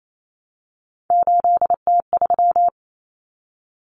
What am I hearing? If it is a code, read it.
Morse code, heard as a keyed tone, about 28 wpm, 700 Hz. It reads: 8T3
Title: Common Cut Numbers - 3-Digit 28wpm